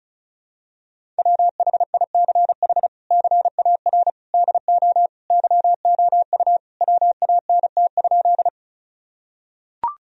Morse code, heard as a keyed tone, about 36 wpm, 700 Hz. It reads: WHICH CAR DO YOU WANT?
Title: Sentences from Top 1000 Words - Review Part 1 - 35wpm